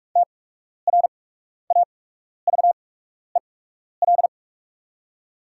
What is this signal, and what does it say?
Morse code, heard as a keyed tone, about 44 wpm, spaced as for 12 wpm, 690 Hz. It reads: TRAVEL
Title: 8x Character Spacing - Top 500 Words - 45wpm